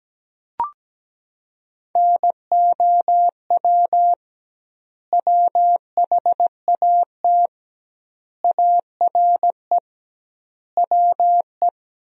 Morse code, heard as a keyed tone, about 17 wpm, 700 Hz. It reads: NOW WHAT ARE WE
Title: 2x Word-Spacing - Sentences from Top 100 Words - 17wpm